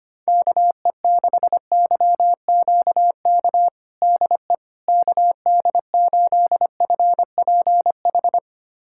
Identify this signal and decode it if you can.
Morse code, heard as a keyed tone, about 25 wpm, 700 Hz. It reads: KE6YQK DE KD8FPH
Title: USA Callsign DE Callsign 25wpm